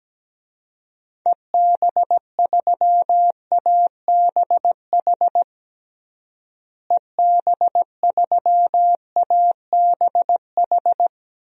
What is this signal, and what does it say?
Morse code, heard as a keyed tone, about 17 wpm, 700 Hz. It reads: EB3ABH EB3ABH
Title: DXCC-100 Call Signs - 6 Characters - Repeated 3x - 17wpm